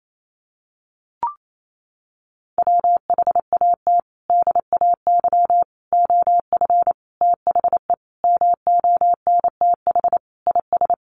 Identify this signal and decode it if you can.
Morse code, heard as a keyed tone, about 28 wpm, 700 Hz. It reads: WHAT DAY OF THE MONTH IS
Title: Sentences from Top 800 Words 28wpm